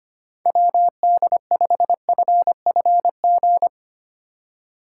Morse code, heard as a keyed tone, about 25 wpm, 700 Hz. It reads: WD5FFG
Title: USA Call Signs - 6 Characters - 4x Character Spacing - 25wpm